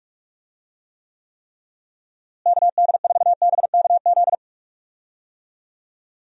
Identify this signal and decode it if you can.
Morse code, heard as a keyed tone, about 46 wpm, 700 Hz. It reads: KD4BKZ
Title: USA Call Signs - 6 Characters - Repeated 3x - 45wpm